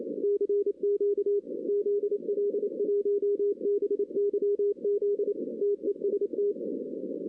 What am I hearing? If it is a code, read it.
Morse code, heard as a keyed tone, about 27 wpm, 410 Hz. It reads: CQ ZL1BYZ TEST
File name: 2016 arrl dx cw / 2016-ARRL-DX-CW-K5TR- - 0-9